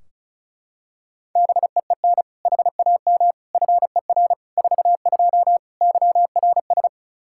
Morse code, heard as a keyed tone, about 34 wpm, 700 Hz. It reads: BEEN HAM FER 42 YRS
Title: Practice QSO - Exchange 3 - More Info - 35wpm